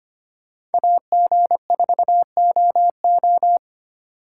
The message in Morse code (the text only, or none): AG4OO